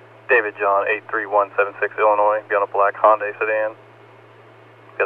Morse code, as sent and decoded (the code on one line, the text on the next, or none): none